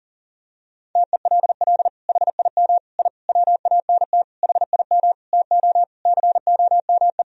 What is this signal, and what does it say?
Morse code, heard as a keyed tone, about 40 wpm, 700 Hz. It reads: TELL HIM I WANT HIM TO COME
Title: Sentences from Top 1000 Words - Review Part 2 - 40wpm